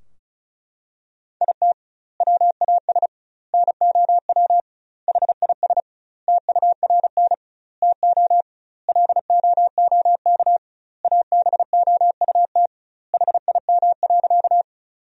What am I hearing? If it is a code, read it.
Morse code, heard as a keyed tone, about 36 wpm, 700 Hz. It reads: IT WAS NOW HIS TURN TO LOOK ABOUT HIM.
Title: 2x Word-Spacing - Sentences from Top 200 Words - 35wpm